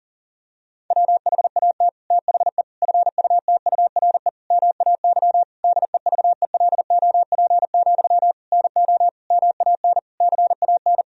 Morse code, heard as a keyed tone, about 40 wpm, 700 Hz. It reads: WHAT THE FUTURE MAY DEVELOP, NO MAN CAN
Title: Sentences from Top 1000 Words - Review Part 2 - 40wpm